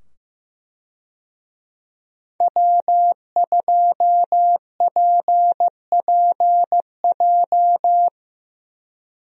Morse code, heard as a keyed tone, about 15 wpm, 700 Hz. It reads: W2PPJ
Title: USA Call Signs - 5 Characters - 15wpm